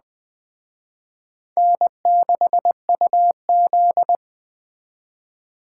Morse code, heard as a keyed tone, about 20 wpm, 700 Hz. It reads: N6UZ
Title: Speed Racing - USA Call Signs 17wpm